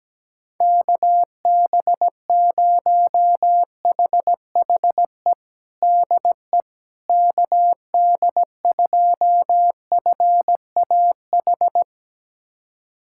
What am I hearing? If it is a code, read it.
Morse code, heard as a keyed tone, about 17 wpm, 700 Hz. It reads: KB0HHE DE KD2FAH
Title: USA Callsign DE Callsign 17wpm